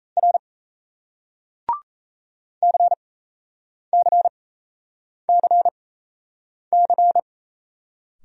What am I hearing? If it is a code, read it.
Morse code, heard as a keyed tone, about 38 wpm, spaced as for 8 wpm, 700 Hz. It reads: R CCCC